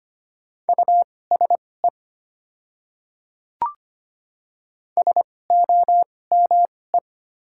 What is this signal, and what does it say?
Morse code, heard as a keyed tone, about 25 wpm, spaced as for 18 wpm, 690 Hz. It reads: USE SOME
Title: 2x Character Spacing - Top 100 Words - 25wpm